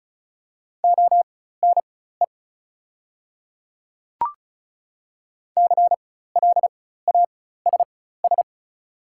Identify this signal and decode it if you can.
Morse code, heard as a keyed tone, about 36 wpm, spaced as for 16 wpm, 690 Hz. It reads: ONE CLASS
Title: 4x Character Spacing - Top 500 Words - 35wpm